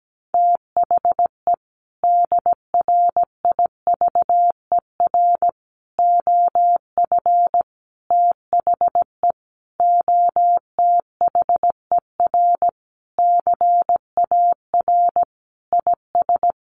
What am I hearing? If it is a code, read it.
Morse code, heard as a keyed tone, about 17 wpm, 700 Hz. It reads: THE DRIVER OF THE OTHER CAR IS
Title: Sentences from Top 1000 Words 17wpm